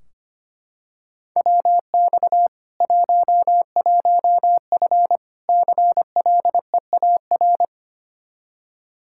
Morse code, heard as a keyed tone, about 25 wpm, 700 Hz. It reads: WX 11F CLEAR
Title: Practice QSO - Exchange 2 - Info - 25wpm